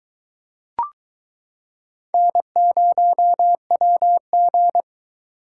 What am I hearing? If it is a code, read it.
Morse code, heard as a keyed tone, about 22 wpm, 700 Hz. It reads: N0WG